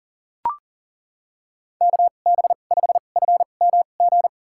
Morse code, heard as a keyed tone, about 37 wpm, spaced as for 28 wpm, 700 Hz. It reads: KB5FMG
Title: USA Call Signs - 6 Characters - 2x Character Spacing - 40wpm